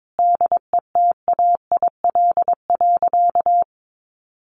Morse code, heard as a keyed tone, about 22 wpm, 700 Hz. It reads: DETAIL.
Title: Sentences from Top 1500 Words 22wpm